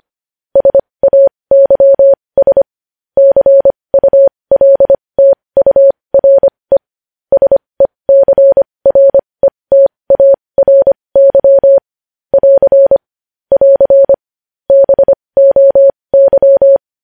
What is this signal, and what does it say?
Morse code, heard as a keyed tone, about 25 wpm, spaced as for 20 wpm, 550 Hz. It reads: SAYS CULTURE SECRETARY <AR> <AR> BOY
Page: news headlines in morse code at 20 wpm